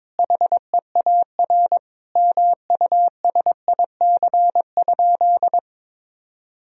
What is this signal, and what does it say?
Morse code, heard as a keyed tone, about 22 wpm, 700 Hz. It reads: HEAR MUSIC?
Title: Sentences from Top 1000 Words - Review Part 1 - 22wpm